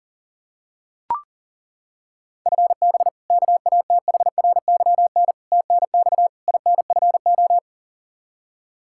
Morse code, heard as a keyed tone, about 40 wpm, 700 Hz. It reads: FB KATHRYN TNX INFO
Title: Practice QSO - Exchange 3 - More Info - 40wpm